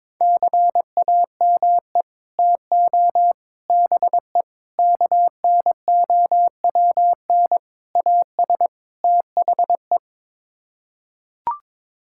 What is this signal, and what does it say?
Morse code, heard as a keyed tone, about 22 wpm, 700 Hz. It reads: CAME TO BE KNOWN AS THE
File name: practice / 22wpm / Sets of 6 Words - 22wpm